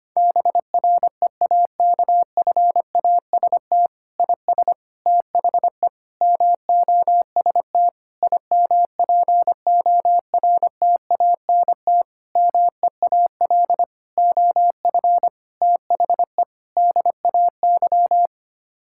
Morse code, heard as a keyed tone, about 25 wpm, 700 Hz. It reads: BREAKFAST IS THE MOST IMPORTANT MEAL OF THE DAY
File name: Sets of 9 Words - 25wpm